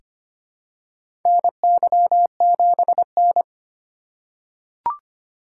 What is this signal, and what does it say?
Morse code, heard as a keyed tone, about 25 wpm, 700 Hz. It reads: NY7N
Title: K1USN SST QSO Practice - 25wpm